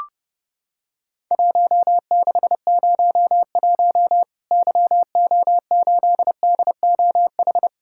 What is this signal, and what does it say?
Morse code, heard as a keyed tone, about 30 wpm, 700 Hz. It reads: 1601 YO8DOH
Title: Worked All Europe DX Contest - QTC Exchange - 30wpm